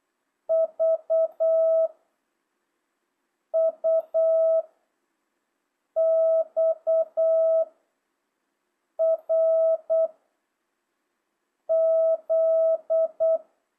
Morse code, heard as a keyed tone, about 8 wpm, 660 Hz. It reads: V U X R Z
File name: jh8cbh / 2025cw